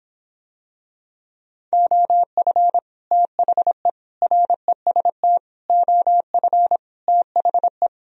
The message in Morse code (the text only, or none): OF THE REST OF THE